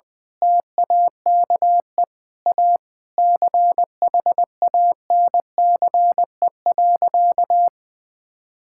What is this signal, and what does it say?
Morse code, heard as a keyed tone, about 20 wpm, 700 Hz. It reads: TAKE A CHANCE.
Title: Sentences from Top 1000 Words - Review Part 6 - 20wpm